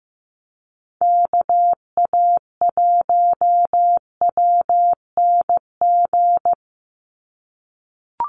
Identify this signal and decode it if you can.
Morse code, heard as a keyed tone, about 15 wpm, 700 Hz. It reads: KA1WNG
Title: USA Call Signs 15wpm